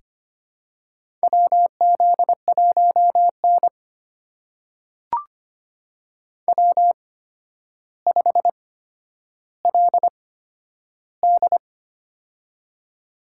Morse code, heard as a keyed tone, about 25 wpm, spaced as for 11 wpm, 700 Hz. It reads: WZ1N W5LD
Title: USA Call Signs - 4 Characters - 8x Character Spacing - 25wpm